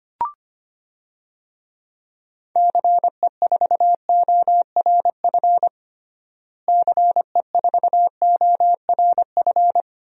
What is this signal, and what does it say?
Morse code, heard as a keyed tone, about 25 wpm, 700 Hz. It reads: CE4ORF CE4ORF
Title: DXCC-100 Call Signs - 6 Characters - Repeated 3x - 25wpm